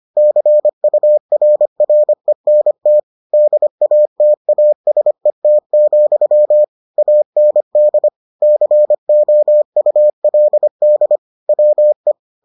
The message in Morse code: CURRENT DATASET, AND COULD WE

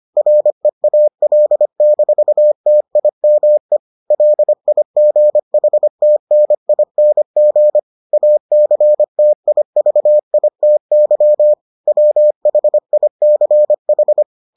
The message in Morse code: REAL-TIME LIGHTNING ACTIVITY WHICH

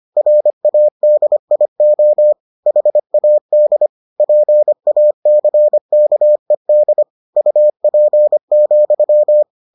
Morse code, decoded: RADIO HAD PACKED UP,

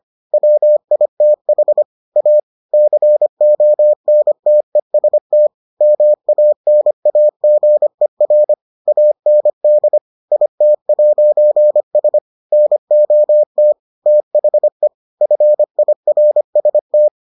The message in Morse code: WITH A CONTEST MANAGER AND IT'S NOT THE FIRST

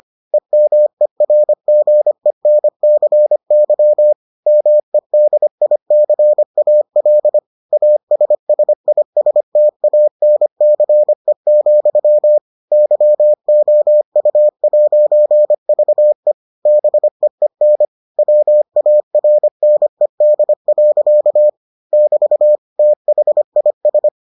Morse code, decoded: EMERGENCY MEDICAL ASSISTANCE, YOU'VE BEEN WARNED. <BT> THIS

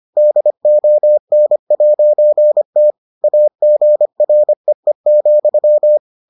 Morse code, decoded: DON'T AGREE,

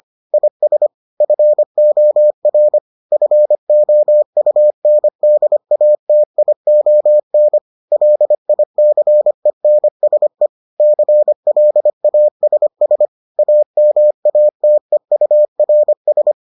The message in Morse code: IS FOR FOUNDATION LICENSE CLASS AMATEURS

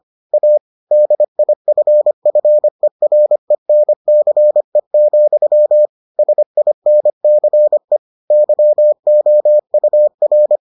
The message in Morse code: A DIFFERENCE, SINCE YOUR

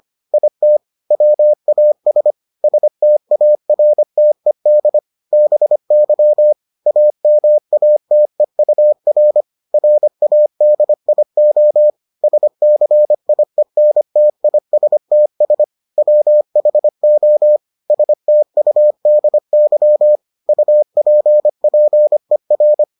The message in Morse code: IT WAS STARTED BY AMATEUR RADIO SCIENTISTS WHO STUDY UPPER

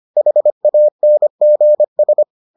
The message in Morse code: HANGS

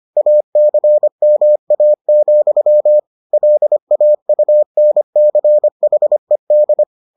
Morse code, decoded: ACMA, LAUNCHED

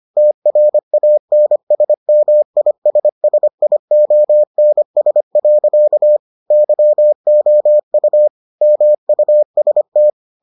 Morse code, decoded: TRANSMISSIONS. YOU MUST

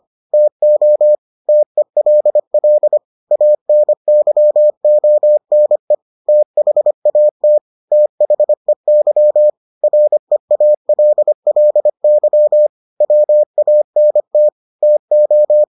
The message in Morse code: TO TELL ANYONE THAT THEY REALLY WANT TO

